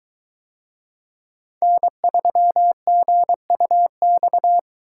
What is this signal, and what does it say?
Morse code, heard as a keyed tone, about 21 wpm, 700 Hz. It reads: N3GUX